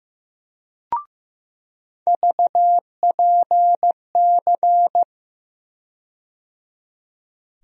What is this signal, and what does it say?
Morse code, heard as a keyed tone, about 15 wpm, 700 Hz. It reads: VPC
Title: USA Call Sign Postfixes - 3 Letters 15wpm